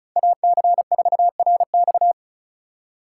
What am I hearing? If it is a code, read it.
Morse code, heard as a keyed tone, about 36 wpm, 700 Hz. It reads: AC4RX
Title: USA Call Signs - 3x Character Spacing - 35wpm